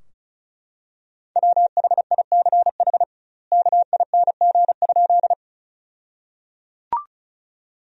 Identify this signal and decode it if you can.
Morse code, heard as a keyed tone, about 34 wpm, 700 Hz. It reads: WHICH KING?
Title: 2x Word-Spacing - Sentences from Top 400 Words - 35wpm